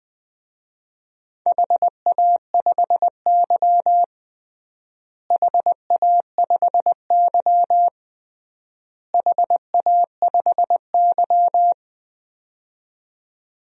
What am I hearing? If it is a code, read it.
Morse code, heard as a keyed tone, about 20 wpm, 700 Hz. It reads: HA5Y HA5Y HA5Y
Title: DXCC-100 Call Signs - 4 Characters - Repeated 3x - 20wpm